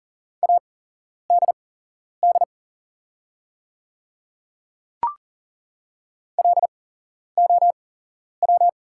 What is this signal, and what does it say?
Morse code, heard as a keyed tone, about 42 wpm, spaced as for 11 wpm, 700 Hz. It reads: ADD LOW
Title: 8x Character Spacing - Top 300 Words - 40wpm